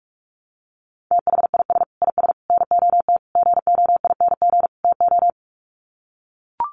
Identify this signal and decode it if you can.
Morse code, heard as a keyed tone, about 44 wpm, 700 Hz. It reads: THIS IS NOT GOING TO
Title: Sets of 5 Words Encore2 45wpm